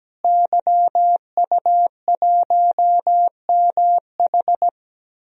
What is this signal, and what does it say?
Morse code, heard as a keyed tone, about 17 wpm, 700 Hz. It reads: YU1MH
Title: DXCC-100 Call Signs - 5 Characters - Repeated 3x - 17wpm